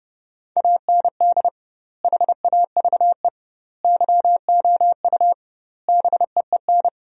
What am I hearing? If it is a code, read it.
Morse code, heard as a keyed tone, about 30 wpm, 700 Hz. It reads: AND HAVE YOU BEEN
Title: 2x Word-Spacing - Sentences from Top 200 Words - 30wpm